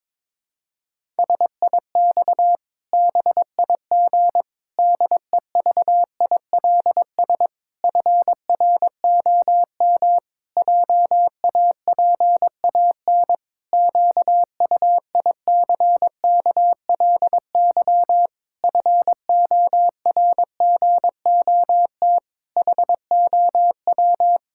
Morse code, heard as a keyed tone, about 22 wpm, 700 Hz. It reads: SIX BIG DEVILS FROM JAPAN QUICKLY FORGOT HOW
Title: Pangram Challenge v2 - 22wpm